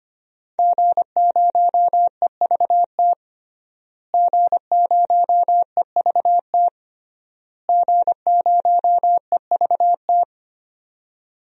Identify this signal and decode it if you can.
Morse code, heard as a keyed tone, about 25 wpm, 700 Hz. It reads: G0EVT G0EVT G0EVT
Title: DXCC-100 Call Signs - 5 Characters - Repeated 3x - 25wpm